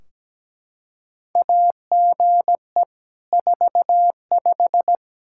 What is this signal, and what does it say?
Morse code, heard as a keyed tone, about 17 wpm, 700 Hz. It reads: AGE 45
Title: Practice QSO - Exchange 3 - More Info - 17wpm